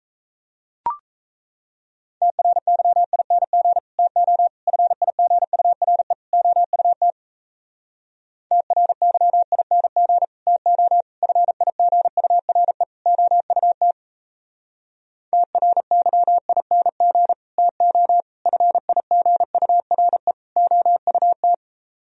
Morse code, heard as a keyed tone, about 38 wpm, 700 Hz. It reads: TRYING TO FIGURE OUT TRYING TO FIGURE OUT TRYING TO FIGURE OUT